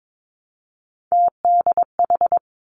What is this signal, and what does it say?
Morse code, heard as a keyed tone, about 22 wpm, 700 Hz. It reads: TDH
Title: Three Challenging Letter-Number Combinations 22wpm